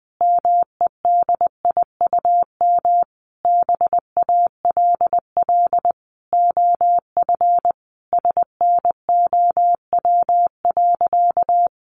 Morse code, heard as a keyed tone, about 19 wpm, 700 Hz. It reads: MEDIUM BALL OF SNOW.
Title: Easy Phrases 20wpm